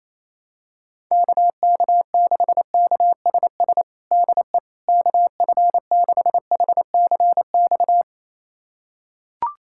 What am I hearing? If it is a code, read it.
Morse code, heard as a keyed tone, about 28 wpm, 700 Hz. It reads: KK6KSS DE KF6HCX
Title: USA Callsign DE Callsign - 6 Characters - 28wpm